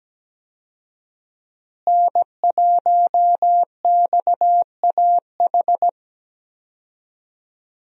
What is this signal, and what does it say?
Morse code, heard as a keyed tone, about 17 wpm, 700 Hz. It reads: N1XAH
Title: USA Call Signs - 5 Characters - Repeated 2x - 17wpm